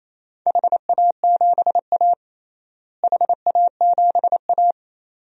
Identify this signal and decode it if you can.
Morse code, heard as a keyed tone, about 28 wpm, 700 Hz. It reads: HA7A HA7A
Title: DXCC-100 Call Signs - 4 Characters - Repeated 2x - 28wpm